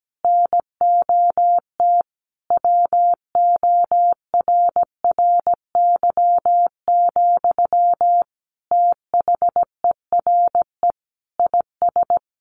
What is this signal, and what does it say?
Morse code, heard as a keyed tone, about 17 wpm, 700 Hz. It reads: NOT WORRY, THERE IS
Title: Sentences from Top 1300 Words 17wpm